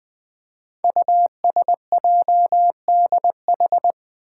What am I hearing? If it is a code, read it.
Morse code, heard as a keyed tone, about 20 wpm, 700 Hz. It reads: USJDH
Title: Code Groups - 5 Letters - 20wpm